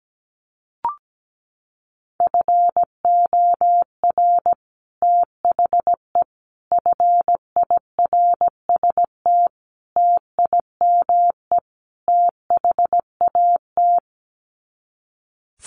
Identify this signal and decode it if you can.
Morse code, heard as a keyed tone, about 17 wpm, 700 Hz. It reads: FOR THE FIRST TIME THAT
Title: Sets of 5 Words Encore 17wpm